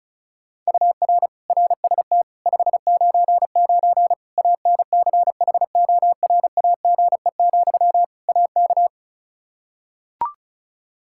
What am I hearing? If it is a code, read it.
Morse code, heard as a keyed tone, about 36 wpm, 700 Hz. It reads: UR RST 599 ANCHORAGE, AK
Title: K1USN SST QSO Practice 35wpm